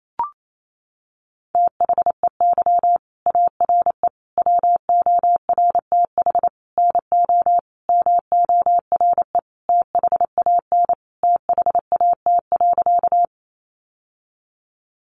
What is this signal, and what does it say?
Morse code, heard as a keyed tone, about 28 wpm, 700 Hz. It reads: THEY ARE WORTH NO MORE THAN THAT.
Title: Sentences from Top 800 Words 28wpm